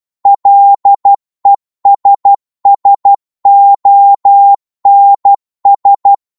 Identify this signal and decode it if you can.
Morse code, heard as a keyed tone, about 12 wpm, 800 Hz. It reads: LESSONS